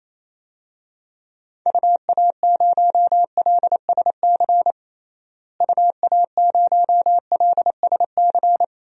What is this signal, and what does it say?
Morse code, heard as a keyed tone, about 28 wpm, 700 Hz. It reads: UA0LSC UA0LSC
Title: DXCC-100 Call Signs - 6 Characters - Repeated 3x - 28wpm